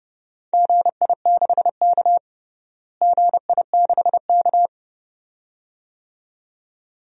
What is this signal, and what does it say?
Morse code, heard as a keyed tone, about 32 wpm, 700 Hz. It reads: GI6K GI6K
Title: DXCC-100 Call Signs - 4 Characters - Repeated 2x - 30wpm